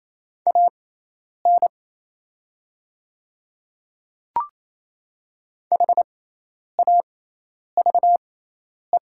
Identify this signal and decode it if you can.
Morse code, heard as a keyed tone, about 28 wpm, spaced as for 10 wpm, 700 Hz. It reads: AN HAVE
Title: 6x Character Spacing - Top 100 Words - 28wpm